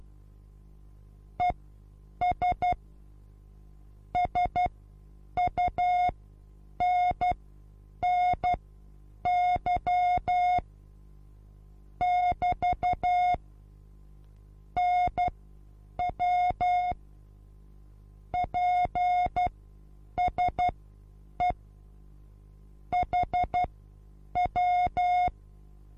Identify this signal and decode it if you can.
Morse code, heard as a keyed tone, about 12 wpm, spaced as for 8 wpm, 720 Hz. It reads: ES SUNNY <BT> NW PSE HW
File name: Onlinelehrgang / morsen / DJ4UF Morsen23